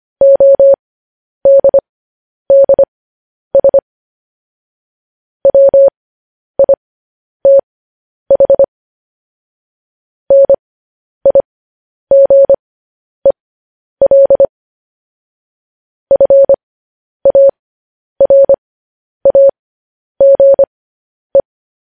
Morse code, heard as a keyed tone, about 25 wpm, spaced as for 10 wpm, 550 Hz. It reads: ODDS WITH NIGEL FARAGE
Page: news headlines in morse code at 10 wpm